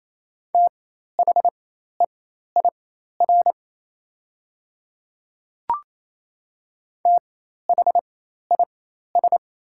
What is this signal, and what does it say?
Morse code, heard as a keyed tone, about 27 wpm, spaced as for 13 wpm, 700 Hz. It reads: THEIR THIS
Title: 4x Character Spacing - Top 100 Words - 28wpm